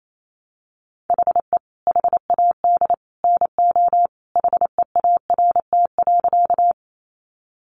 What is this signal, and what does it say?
Morse code, heard as a keyed tone, about 28 wpm, 700 Hz. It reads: HE HAD NO HEART.
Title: Sentences from Top 1200 Words 28wpm